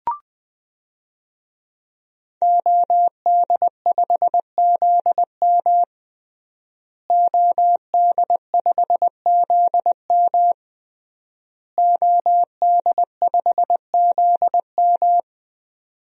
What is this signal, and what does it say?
Morse code, heard as a keyed tone, about 21 wpm, 700 Hz. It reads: OD5ZM OD5ZM OD5ZM
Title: DXCC-100 Call Signs - 5 Characters - Repeated 3x - 20wpm